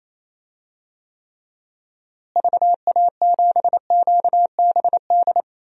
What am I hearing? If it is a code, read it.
Morse code, heard as a keyed tone, about 28 wpm, 700 Hz. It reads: VA7QBD
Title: DXCC-100 Call Signs - 6 Characters - Repeated 3x - 28wpm